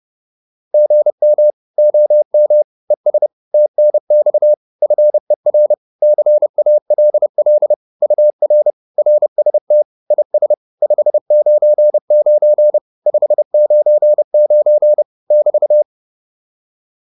Morse code, heard as a keyed tone, about 30 wpm, 600 Hz. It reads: GM OM ES TNX FER CALL UR RST IS 599 599 <BT>
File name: QSO208 30